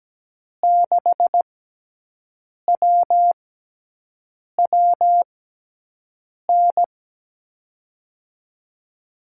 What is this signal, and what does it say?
Morse code, heard as a keyed tone, about 17 wpm, spaced as for 6 wpm, 700 Hz. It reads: 6WWN
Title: USA Call Signs - 5 Characters - 6x Character Spacing - 17wpm